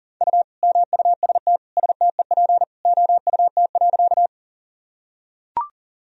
Morse code, heard as a keyed tone, about 40 wpm, 700 Hz. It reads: U MUST STEP OUT.
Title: Sentences from Top 400 Words and Abbreviations 40wpm